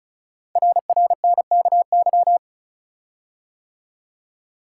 Morse code, heard as a keyed tone, about 36 wpm, 700 Hz. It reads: RRNKY
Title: Code Groups - 5 Letters - 35wpm